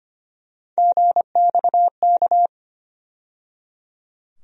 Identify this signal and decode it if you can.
Morse code, heard as a keyed tone, about 25 wpm, 700 Hz. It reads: GXK
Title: USA Call Sign Postfixes - 3 Letters 25wpm